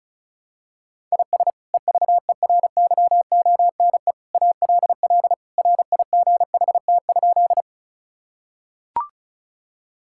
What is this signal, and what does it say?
Morse code, heard as a keyed tone, about 36 wpm, 700 Hz. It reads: IS EVERYONE ALL RIGHT?
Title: Sentences from Top 1000 Words - Review Part 5 - 35wpm